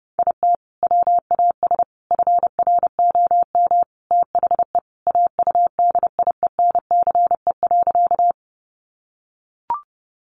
Morse code, heard as a keyed tone, about 30 wpm, 700 Hz. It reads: IT WAS FROM THE AUDIENCE.
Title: Sentences from Top 800 Words 30wpm